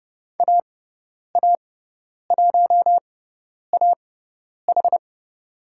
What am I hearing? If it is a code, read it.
Morse code, heard as a keyed tone, about 30 wpm, spaced as for 10 wpm, 700 Hz. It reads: AA1AH
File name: Lesson 18g - Letter m - Call Signs 30wpm